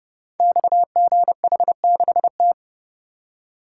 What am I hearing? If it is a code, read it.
Morse code, heard as a keyed tone, about 31 wpm, 700 Hz. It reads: XGH6T
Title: Code Groups - 5 Letter:Numbers - 30wpm